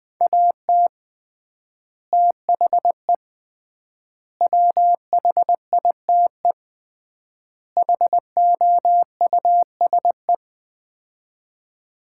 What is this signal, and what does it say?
Morse code, heard as a keyed tone, about 20 wpm, 700 Hz. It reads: AT THE WHITE HOUSE
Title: 3x Word-Spacing - Sets of 4 Words - 20wpm